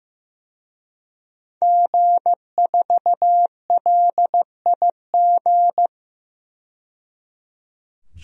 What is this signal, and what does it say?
Morse code, heard as a keyed tone, about 15 wpm, 700 Hz. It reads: G4LIG
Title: DXCC-100 Call Signs - 5 Characters - 15wpm